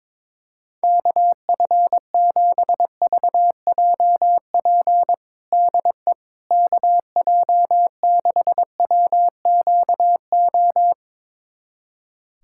Morse code, heard as a keyed tone, about 21 wpm, 700 Hz. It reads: KF7VJP DE KJ6WQO
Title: USA Callsign DE Callsign 22wpm